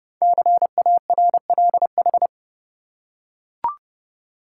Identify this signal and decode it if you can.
Morse code, heard as a keyed tone, about 30 wpm, 700 Hz. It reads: CARLH
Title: Code Groups - 5 Letters - 30wpm